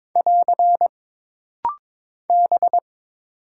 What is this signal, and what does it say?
Morse code, heard as a keyed tone, about 22 wpm, 700 Hz. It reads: <AR> B
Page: This rapid-fire timing is geared to push your ICR (Instant Character Recognition) skills.